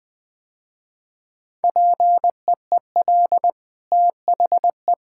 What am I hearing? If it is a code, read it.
Morse code, heard as a keyed tone, about 20 wpm, 700 Hz. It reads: PEEL THE
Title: Easy Sentences with Hints - 20wpm